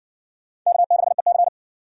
Morse code, heard as a keyed tone, about 58 wpm, 700 Hz. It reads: K6EC